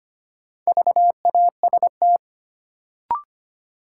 Morse code, heard as a keyed tone, about 25 wpm, 700 Hz. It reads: VAST
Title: Sentences from Top 1300 Words - New Word Drill - 25wpm